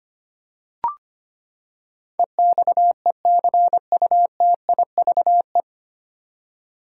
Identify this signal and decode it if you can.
Morse code, heard as a keyed tone, about 25 wpm, 700 Hz. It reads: EXECUTIVE
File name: Nine Letter Words - 25wpm